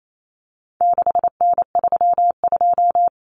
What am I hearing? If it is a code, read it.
Morse code, heard as a keyed tone, about 28 wpm, 700 Hz. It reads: 6N32
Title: Common Cut Numbers - 4-Digit 28wpm